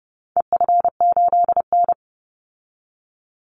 Morse code, heard as a keyed tone, about 30 wpm, 700 Hz. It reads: EF8N
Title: Call Signs Repeated 3x - 30wpm